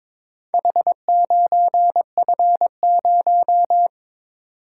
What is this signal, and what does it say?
Morse code, heard as a keyed tone, about 22 wpm, 700 Hz. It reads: H9F0
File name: practice / 22wpm / Four Letter-Number Combinations - 22wpm